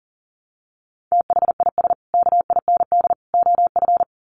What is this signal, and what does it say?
Morse code, heard as a keyed tone, about 40 wpm, 700 Hz. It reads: THIS KIND OF
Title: Sets of 3 Words 40wpm